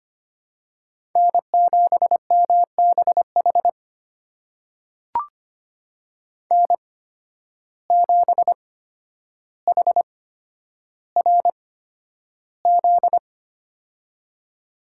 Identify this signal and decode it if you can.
Morse code, heard as a keyed tone, about 25 wpm, spaced as for 11 wpm, 700 Hz. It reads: N7MBH N7HRZ
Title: USA Call Signs - 5 Characters - 8x Character Spacing - 25wpm